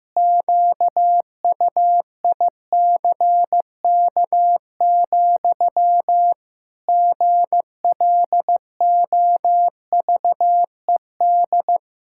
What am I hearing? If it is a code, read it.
Morse code, heard as a keyed tone, about 16 wpm, 700 Hz. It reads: QUICK, GLOVED
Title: Pangram Challenge v2 - 15wpm